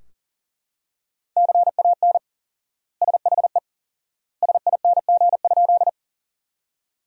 Morse code, heard as a keyed tone, about 40 wpm, 700 Hz. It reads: CAN SHE SING?
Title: 4x Word-Spacing - Sentences from Top 400 Words - 40wpm